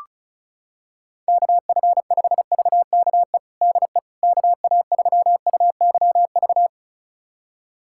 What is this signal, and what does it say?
Morse code, heard as a keyed tone, about 34 wpm, 700 Hz. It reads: KF5VKE DE KA3UYV
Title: USA Callsign DE Callsign - 6 Characters - 35wpm